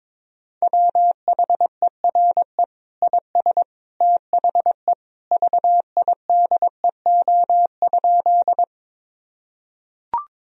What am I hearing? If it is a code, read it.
Morse code, heard as a keyed tone, about 22 wpm, 700 Hz. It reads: WHERE IS THE VIDEO?
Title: Sentences from Top 1000 Words - Review Part 4 - 22wpm